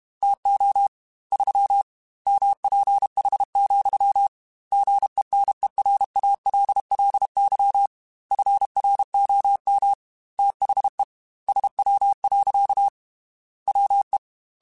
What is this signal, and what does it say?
Morse code, heard as a keyed tone, about 32 wpm, 800 Hz. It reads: TO 3 MPH, GENERALLY FROM THE SW. WE
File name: Weather Sets / Weather Sets-mp3s / Weather 5 5 @30